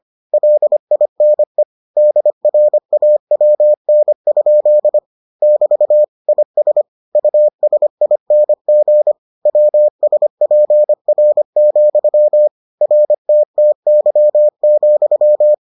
Morse code, heard as a keyed tone, about 25 wpm, 600 Hz. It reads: LINE DRAWN? <BT> IS USING WSPR, RTTY,